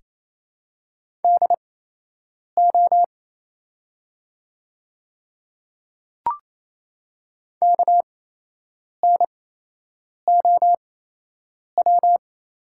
Morse code, heard as a keyed tone, about 28 wpm, spaced as for 8 wpm, 700 Hz. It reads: DO KNOW
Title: 8x Character Spacing - Top 100 Words - 28wpm